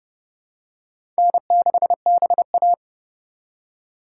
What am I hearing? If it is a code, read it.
Morse code, heard as a keyed tone, about 30 wpm, 700 Hz. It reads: N6BA
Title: USA Call Signs - 4 Characters - 4x Character Spacing - 30wpm